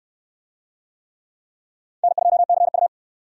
Morse code, heard as a keyed tone, about 60 wpm, 700 Hz. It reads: N2CU